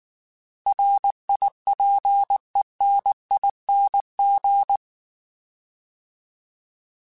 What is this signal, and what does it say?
Morse code, heard as a keyed tone, about 19 wpm, 800 Hz. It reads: RIPENING
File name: Alphabetical / Alphabetical mp3s / Rs @18